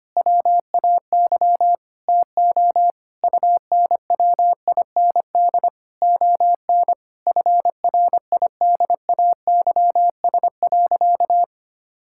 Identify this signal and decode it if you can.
Morse code, heard as a keyed tone, about 25 wpm, 700 Hz. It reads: WAY TO UNWIND ON FRIDAYS.
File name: practice / 25wpm / Difficult Sentences with Hints - 25wpm